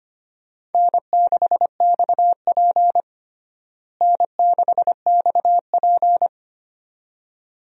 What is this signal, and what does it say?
Morse code, heard as a keyed tone, about 25 wpm, 700 Hz. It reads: N6XP N6XP
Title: USA Call Signs - 4 Characters - Repeated 2x - 25wpm